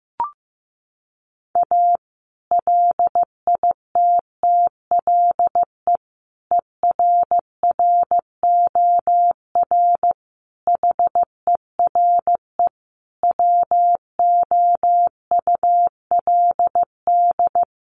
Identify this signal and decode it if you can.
Morse code, heard as a keyed tone, about 15 wpm, 700 Hz. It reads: A LITTLE ERROR HERE WOULD
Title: A Sound Of Thunder 15wpm